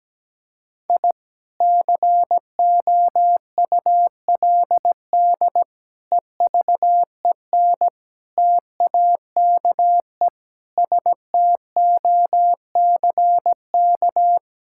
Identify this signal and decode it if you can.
Morse code, heard as a keyed tone, about 17 wpm, 700 Hz. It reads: I COULD EVEN TAKE STOCK
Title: Sentences from Top 1000 Words - Review Part 5 - 17wpm